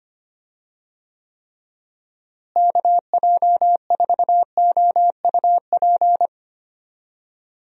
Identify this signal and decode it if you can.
Morse code, heard as a keyed tone, about 25 wpm, 700 Hz. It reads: KJ4OUP
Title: USA Call Signs - 6 Characters - Repeated 3x - 25wpm